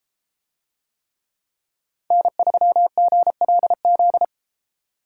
Morse code, heard as a keyed tone, about 32 wpm, 700 Hz. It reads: N3GLZ